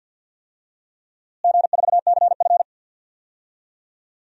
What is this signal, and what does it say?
Morse code, heard as a keyed tone, about 53 wpm, 700 Hz. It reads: G4CF
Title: DXCC-100 Call Signs - 4 Characters - Repeated 3x - 50wpm